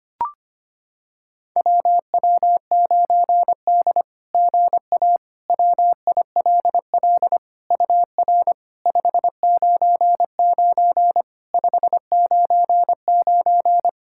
Morse code, heard as a keyed tone, about 25 wpm, 690 Hz. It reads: WW9D GA WILL UR 599 599
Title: SOTA QSO Practice 25wpm